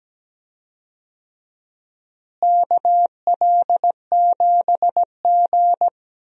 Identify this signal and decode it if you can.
Morse code, heard as a keyed tone, about 17 wpm, 700 Hz. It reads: KL7G
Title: DXCC-100 Call Signs - 4 Characters - Repeated 3x - 17wpm